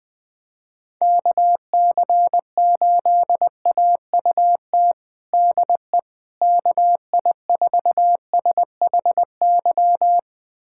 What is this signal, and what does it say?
Morse code, heard as a keyed tone, about 20 wpm, 700 Hz. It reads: KC8AUT DE KI4SHY
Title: USA Callsign DE Callsign - 6 Characters - 20wpm